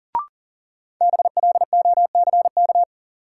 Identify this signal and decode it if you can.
Morse code, heard as a keyed tone, about 40 wpm, 700 Hz. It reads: BLOCK
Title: Sentences from Top 1600 Words - New Word Drill - 40wpm